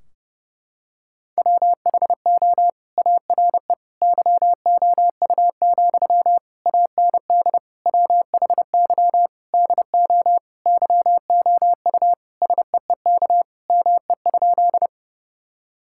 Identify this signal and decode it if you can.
Morse code, heard as a keyed tone, about 32 wpm, 700 Hz. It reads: WHO ARE YOU, AND WHY DO YOU SEEK ME?
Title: Sentences from Top 1600 Words 30wpm